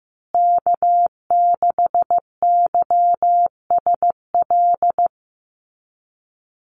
Morse code, heard as a keyed tone, about 15 wpm, 700 Hz. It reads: K6YSL
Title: USA Call Signs Repeated 2x 15wpm